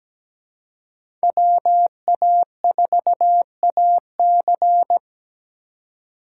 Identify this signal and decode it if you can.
Morse code, heard as a keyed tone, about 20 wpm, 700 Hz. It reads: WA4AC
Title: Speed Racing - USA 2x2 Call Signs 17wpm